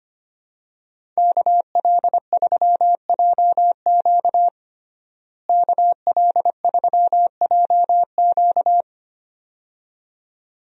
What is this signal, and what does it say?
Morse code, heard as a keyed tone, about 25 wpm, 700 Hz. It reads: KL3JQ KL3JQ
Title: USA Call Signs - 5 Characters - Repeated 2x - 25wpm